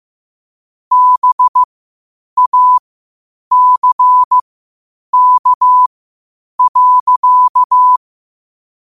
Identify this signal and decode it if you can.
Morse code, heard as a keyed tone, about 15 wpm, spaced as for 8 wpm, 1000 Hz. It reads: BACK.